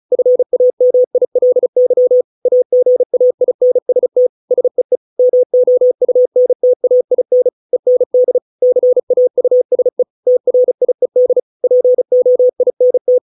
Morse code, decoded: FAMILY AGAINST SEE MOUNTAIN END CAUSE TRIED POINT

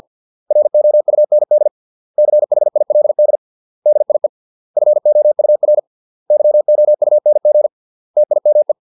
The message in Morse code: ROUND CHILD DIE FOUR YOUNG TIME

